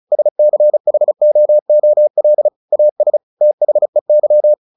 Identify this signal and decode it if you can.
Morse code, heard as a keyed tone, about 34 wpm, 600 Hz. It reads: SCHOOL AS THEY
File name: the old man and the sea 2 35wpm